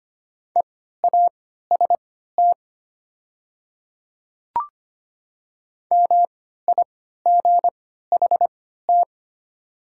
Morse code, heard as a keyed tone, about 25 wpm, spaced as for 14 wpm, 700 Hz. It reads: EAST MIGHT
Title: 3x Character Spacing - Top 500 Words - 25wpm